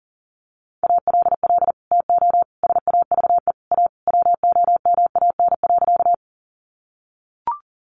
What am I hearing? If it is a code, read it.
Morse code, heard as a keyed tone, about 40 wpm, 700 Hz. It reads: ALL TO SAVE A WOMAN.
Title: Sentences from Top 1100 Words 40wpm